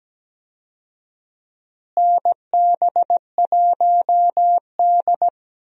Morse code, heard as a keyed tone, about 17 wpm, 700 Hz. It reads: NB1D
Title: USA Call Signs - 4 Characters - Repeated 3x - 17wpm